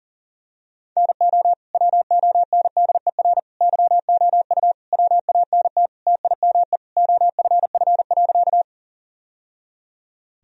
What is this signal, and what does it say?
Morse code, heard as a keyed tone, about 37 wpm, 700 Hz. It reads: NO WONDER YOU WANT TIME OFF.
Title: Sentences from Top 1000 Words - Review Part 3 - 40wpm